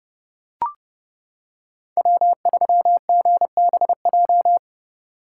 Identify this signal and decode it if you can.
Morse code, heard as a keyed tone, about 29 wpm, 700 Hz. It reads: W3GBJ